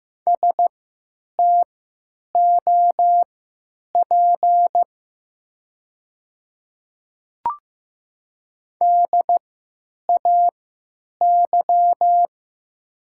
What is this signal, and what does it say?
Morse code, heard as a keyed tone, about 15 wpm, spaced as for 9 wpm, 700 Hz. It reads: STOP DAY
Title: 3x Character Spacing - Top 300 Words - 15wpm